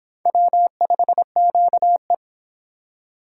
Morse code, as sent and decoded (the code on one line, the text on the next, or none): .-- ..... --.- .
W5QE